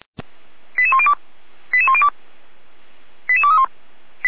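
Five-tone selective call (ZVEI), heard as quick runs of five tones, tones of about 70 ms. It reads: 88102 88102 88231
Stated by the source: ZVEI five-tone selective call 88149 received by SDR.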